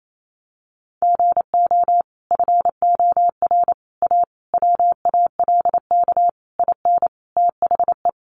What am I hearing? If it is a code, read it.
Morse code, heard as a keyed tone, about 28 wpm, 700 Hz. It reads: GO FOR A WALK IN THE
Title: Sentences from Top 1300 Words 28wpm